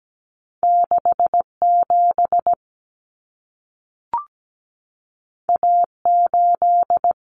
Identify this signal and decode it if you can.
Morse code, heard as a keyed tone, about 17 wpm, 700 Hz. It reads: 67 A8
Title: Common Cut Numbers - 2-Digit 17wpm